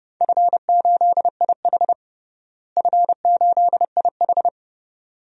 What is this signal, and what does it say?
Morse code, heard as a keyed tone, about 30 wpm, 700 Hz. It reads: F8IH F8IH
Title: DXCC-100 Call Signs - 4 Characters - Repeated 2x - 30wpm